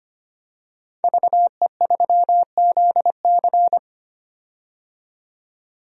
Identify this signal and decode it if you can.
Morse code, heard as a keyed tone, about 25 wpm, 700 Hz. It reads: VE3ZC
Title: DXCC-100 Call Signs - 5 Characters - 25wpm